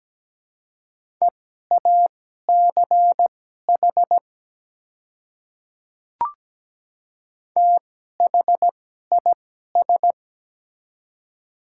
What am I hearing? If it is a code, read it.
Morse code, heard as a keyed tone, about 17 wpm, spaced as for 12 wpm, 700 Hz. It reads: EACH THIS
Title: 2x Character Spacing - Top 100 Words - 17wpm